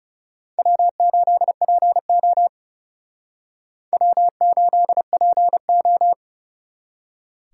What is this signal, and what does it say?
Morse code, heard as a keyed tone, about 35 wpm, 700 Hz. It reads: W8PO W8PO
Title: Speed Racing - USA 1x2 Call Signs 30wpm